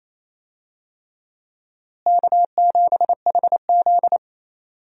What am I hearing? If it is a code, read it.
Morse code, heard as a keyed tone, about 28 wpm, 700 Hz. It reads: K7HZ
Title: USA Call Signs - 4 Characters - Repeated 3x - 28wpm